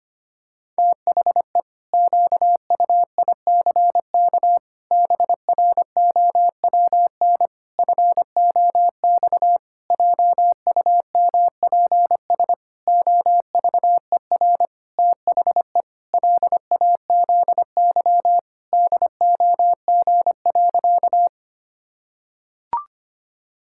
Morse code, heard as a keyed tone, about 25 wpm, 700 Hz. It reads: THE QUICK BROWN FOX JUMPS OVER THE LAZY DOG.
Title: Pangram Challenge v2 - 25wpm